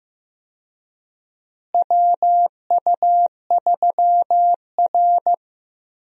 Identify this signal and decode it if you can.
Morse code, heard as a keyed tone, about 15 wpm, 700 Hz. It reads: WU3R
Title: USA Call Signs - 4 Characters - Repeated 2x - 15wpm